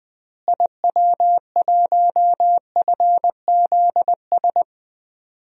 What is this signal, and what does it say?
Morse code, heard as a keyed tone, about 20 wpm, 700 Hz. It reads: IW1FZS
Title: DXCC-100 Call Signs - 6 Characters - Repeated 2x - 20wpm